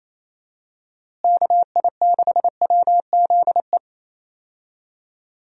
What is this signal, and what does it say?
Morse code, heard as a keyed tone, about 28 wpm, 700 Hz. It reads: KI6WZE
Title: USA Call Signs - 6 Characters - Repeated 2x - 28wpm